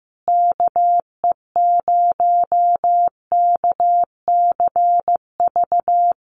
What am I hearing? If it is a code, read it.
Morse code, heard as a keyed tone, about 14 wpm, 700 Hz. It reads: KE0KCV
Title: USA Call Signs Repeated 3x 15wpm